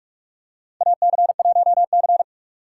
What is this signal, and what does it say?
Morse code, heard as a keyed tone, about 46 wpm, 700 Hz. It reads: AC1C
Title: USA Call Signs - 4 Characters - 6x Character Spacing - 45wpm